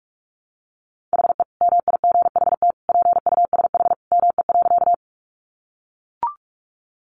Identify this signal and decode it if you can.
Morse code, heard as a keyed tone, about 44 wpm, 700 Hz. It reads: HE MIGHT PUSH ME.
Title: Sentences from Top 1300 Words 45wpm